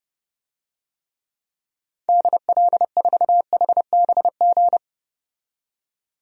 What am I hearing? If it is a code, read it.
Morse code, heard as a keyed tone, about 31 wpm, 700 Hz. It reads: DL4HBG
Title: DXCC-100 Call Signs - 6 Characters - 30wpm